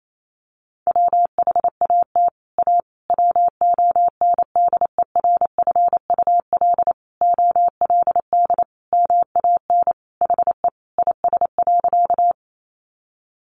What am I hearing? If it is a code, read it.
Morse code, heard as a keyed tone, about 28 wpm, 700 Hz. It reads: WHAT A WONDERFUL OLD MAN HE IS.
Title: Sentences from Top 1100 Words 28wpm